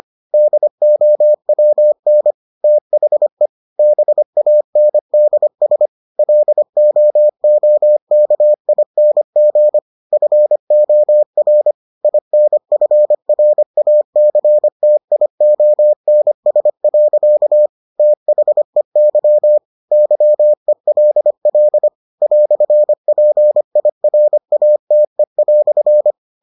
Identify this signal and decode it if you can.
Morse code, heard as a keyed tone, about 25 wpm, 600 Hz. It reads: DOWN THE BANDS LOOKING FOR INFRACTIONS. THEY YELL "PIRATE"